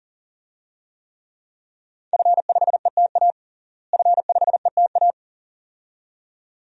Morse code, heard as a keyed tone, about 40 wpm, 700 Hz. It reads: F5ETA F5ETA
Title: DXCC-100 Call Signs - 5 Characters - Repeated 2x - 40wpm